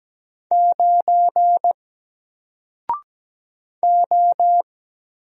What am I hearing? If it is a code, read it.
Morse code, heard as a keyed tone, about 17 wpm, 700 Hz. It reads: 9 O
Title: Single Letter-Number 17wpm